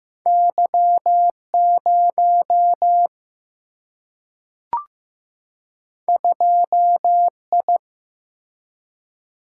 Morse code, heard as a keyed tone, about 15 wpm, 700 Hz. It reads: Y 0 2 I
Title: Single-Letter Single-Number 15wpm